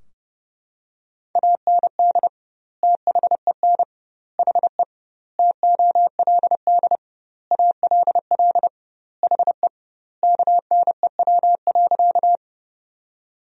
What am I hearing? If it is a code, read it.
Morse code, heard as a keyed tone, about 32 wpm, 700 Hz. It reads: AND THEN HE TOLD ALL HE KNEW.
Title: 2x Word-Spacing - Sentences from Top 400 Words - 30wpm